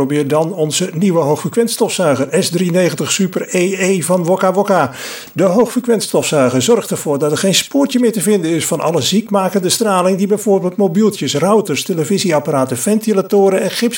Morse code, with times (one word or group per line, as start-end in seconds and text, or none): none